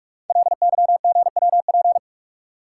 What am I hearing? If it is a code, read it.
Morse code, heard as a keyed tone, about 44 wpm, 700 Hz. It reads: LYGWP
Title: Code Groups - 5 Letter:Numbers - 45wpm